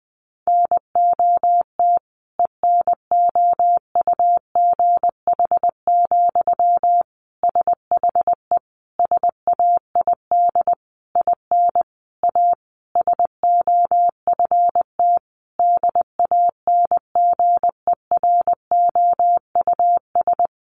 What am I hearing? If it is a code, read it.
Morse code, heard as a keyed tone, about 19 wpm, 700 Hz. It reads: NOT ENOUGH, SHE SAID IN A SOFT DANGEROUS
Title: Sentences from Top 1500 Words 20wpm